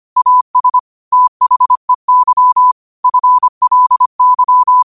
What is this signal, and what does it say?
Morse code, heard as a keyed tone, about 25 wpm, 1000 Hz. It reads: AS THEY FLY
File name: qotdfiles / 25wpm